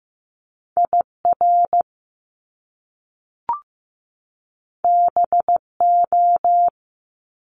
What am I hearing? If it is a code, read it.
Morse code, heard as a keyed tone, about 15 wpm, 700 Hz. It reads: IR BO
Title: Top Two Letter Combinations in Words 15wpm